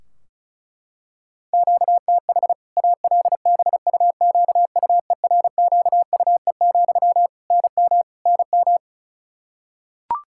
Morse code, heard as a keyed tone, about 34 wpm, 700 Hz. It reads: QTH ALBUQUERQUE, NM NM
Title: Practice QSO - Exchange 1 - Beginning - 35wpm